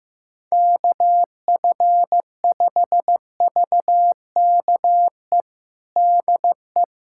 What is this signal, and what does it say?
Morse code, heard as a keyed tone, about 16 wpm, 700 Hz. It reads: KF5VKE DE
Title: USA Callsign DE Callsign - 6 Characters - 15wpm